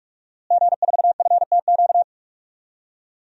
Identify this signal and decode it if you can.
Morse code, heard as a keyed tone, about 42 wpm, 700 Hz. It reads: G4FTQ
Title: DXCC-100 Call Signs - 5 Characters - Repeated 2x - 45wpm